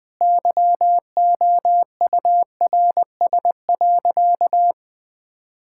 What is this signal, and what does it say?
Morse code, heard as a keyed tone, about 20 wpm, 700 Hz. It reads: YOURS.
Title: Sentences with 8-Word Phrases - 20wpm